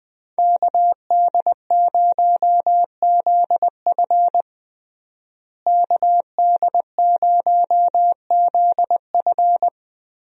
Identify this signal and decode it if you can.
Morse code, heard as a keyed tone, about 20 wpm, 700 Hz. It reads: KD0ZF KD0ZF
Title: USA Call Signs - 5 Characters - Repeated 3x - 20wpm